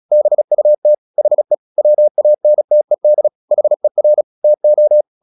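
Morse code, heard as a keyed tone, about 36 wpm, 600 Hz. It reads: BUT HE WANTED HER TO
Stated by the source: Morse practice at 36 wpm, text of Ken Follett's novel Never, from CW med Gnister.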